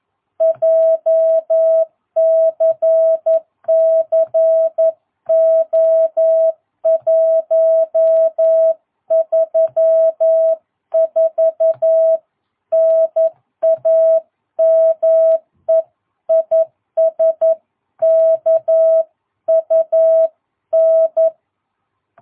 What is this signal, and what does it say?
Morse code, heard as a keyed tone, about 11 wpm, 660 Hz. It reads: JCC O134 NAME IS KUN
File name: jh8cbh / 2022cw